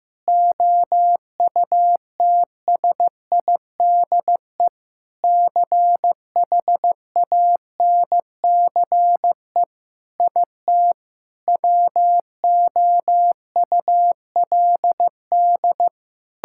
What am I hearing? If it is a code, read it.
Morse code, heard as a keyed tone, about 15 wpm, 700 Hz. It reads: OUTSIDE CHANCE IT WOULD
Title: Sentences from Top 1000 Words - Review Part 6 - 15wpm